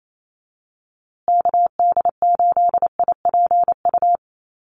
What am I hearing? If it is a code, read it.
Morse code, heard as a keyed tone, about 28 wpm, 700 Hz. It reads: KD8IPU
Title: USA Call Signs Repeated 2x 28wpm